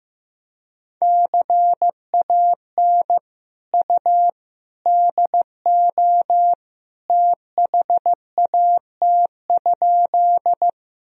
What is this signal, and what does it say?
Morse code, heard as a keyed tone, about 16 wpm, 700 Hz. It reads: CAN U DO THAT?
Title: Sentences from Top 100 Words and Abbreviations 15wpm